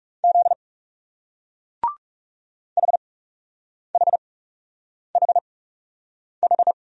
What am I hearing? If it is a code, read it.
Morse code, heard as a keyed tone, about 42 wpm, spaced as for 8 wpm, 700 Hz. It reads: Z HHHH